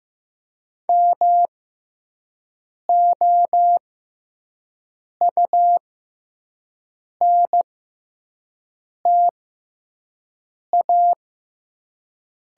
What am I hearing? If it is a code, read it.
Morse code, heard as a keyed tone, about 15 wpm, spaced as for 5 wpm, 700 Hz. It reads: MOUNTA
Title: 6x Character Spacing - Top 300 Words - 15wpm